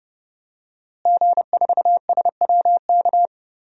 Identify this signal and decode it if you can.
Morse code, heard as a keyed tone, about 30 wpm, 700 Hz. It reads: G4SWK
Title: DXCC-100 Call Signs - 5 Characters - Repeated 3x - 30wpm